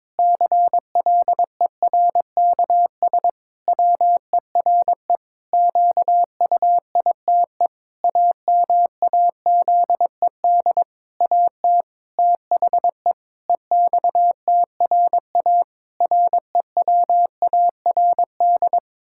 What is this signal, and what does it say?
Morse code, heard as a keyed tone, about 22 wpm, 700 Hz. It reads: CLERKS WERE QUITE AMAZED AT THE EXTRA REWARD
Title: Pangram Challenge v2 - 22wpm